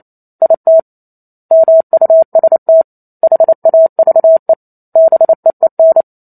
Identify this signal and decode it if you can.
Morse code, heard as a keyed tone, about 28 wpm, 650 Hz. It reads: IT MUST HAVE BEEN